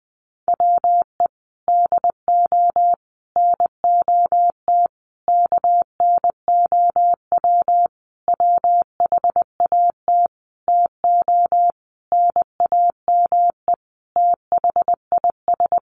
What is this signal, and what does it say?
Morse code, heard as a keyed tone, about 20 wpm, 700 Hz. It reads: WE DO NOT KNOW WHAT TO NAME THIS
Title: Sentences from Top 1500 Words 20wpm